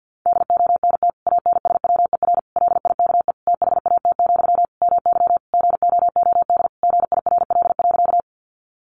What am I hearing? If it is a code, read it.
Morse code, heard as a keyed tone, about 50 wpm, 700 Hz. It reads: DONT ANSWER LIKE THAT, MY GOOD GIRL.
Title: Sentences from Top 400 Words 50wpm